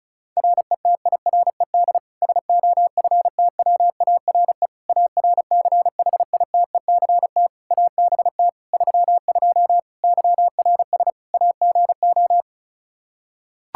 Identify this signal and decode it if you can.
Morse code, heard as a keyed tone, about 36 wpm, 700 Hz. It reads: RETIRED SOFTWARE ARCHITECT ABT 32 YRS AGO
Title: Practice QSO - Exchange 3 - More Info - 35wpm